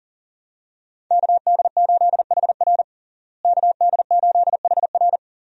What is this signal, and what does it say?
Morse code, heard as a keyed tone, about 39 wpm, 700 Hz. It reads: KD8HR KD8HR
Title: USA Call Signs - 5 Characters - Repeated 3x - 40wpm